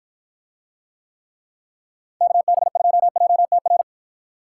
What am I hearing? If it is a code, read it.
Morse code, heard as a keyed tone, about 53 wpm, 700 Hz. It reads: KB2JTR